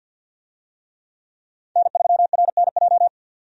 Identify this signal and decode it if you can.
Morse code, heard as a keyed tone, about 48 wpm, 700 Hz. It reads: N3RNJ
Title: USA Call Signs - 5 Characters - 50wpm